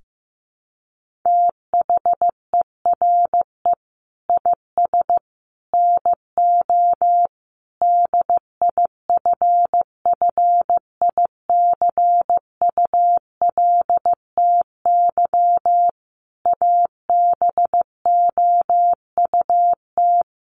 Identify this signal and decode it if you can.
Morse code, heard as a keyed tone, about 15 wpm, 700 Hz. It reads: THERE IS NO DIFFICULTY ABOUT
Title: Sentences from Top 1500 Words 15wpm